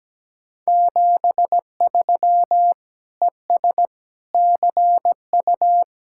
Morse code, heard as a keyed tone, about 19 wpm, 700 Hz. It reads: 73 ES CU
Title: Speed-Racing Common QSO Phrases - 17wpm